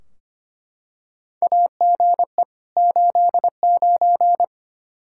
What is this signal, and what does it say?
Morse code, heard as a keyed tone, about 25 wpm, 700 Hz. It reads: AGE 89
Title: Practice QSO - Exchange 3 - More Info - 25wpm